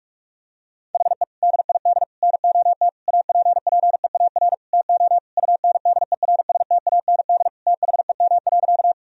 Morse code, heard as a keyed tone, about 46 wpm, 700 Hz. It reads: HE DID NOT APPEAR TO UNDERSTAND THEM.
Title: Sentences from Top 1000 Words - Review Part 2 - 45wpm